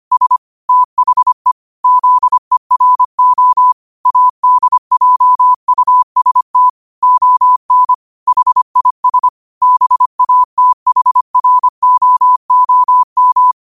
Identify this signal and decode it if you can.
Morse code, heard as a keyed tone, about 25 wpm, 1000 Hz. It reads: S THE ZERO ADJUST ON HIS BATHROOM